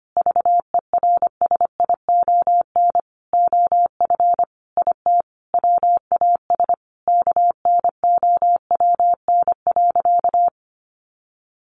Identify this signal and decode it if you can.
Morse code, heard as a keyed tone, about 25 wpm, 700 Hz. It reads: VERSION OF IT WAS KNOWN.
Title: Sentences from Top 1400 Words 25wpm